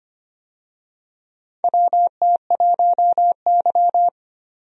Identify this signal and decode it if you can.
Morse code, heard as a keyed tone, about 25 wpm, 700 Hz. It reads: WT1Y
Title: USA Call Signs - 4 Characters - 25wpm